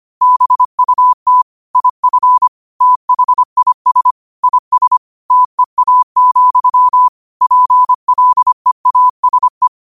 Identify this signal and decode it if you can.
Morse code, heard as a keyed tone, about 25 wpm, 1000 Hz. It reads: BUT IF THIS IS TEA, PLEASE